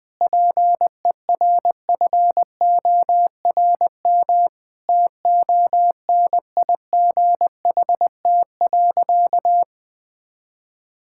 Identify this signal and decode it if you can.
Morse code, heard as a keyed tone, about 21 wpm, 700 Hz. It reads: PERFORM TONIGHT.
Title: Sentences from Top 1000 Words - Review Part 5 - 20wpm